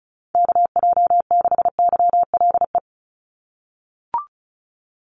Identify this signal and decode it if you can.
Morse code, heard as a keyed tone, about 34 wpm, 700 Hz. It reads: KJ6YLE
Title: USA Call Signs 35wpm